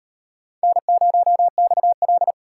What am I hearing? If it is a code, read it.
Morse code, heard as a keyed tone, about 36 wpm, 700 Hz. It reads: N0XL